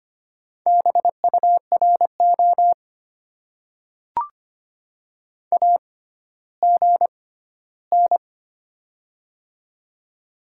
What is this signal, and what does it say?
Morse code, heard as a keyed tone, about 25 wpm, spaced as for 14 wpm, 700 Hz. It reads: BURO AGN
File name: Top 90 QSO Elements - 6x Character Spacing - 25wpm